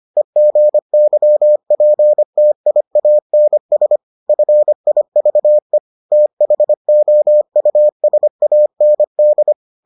Morse code, heard as a keyed tone, about 25 wpm, 600 Hz. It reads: EGYPTIANS FIVE THOUSAND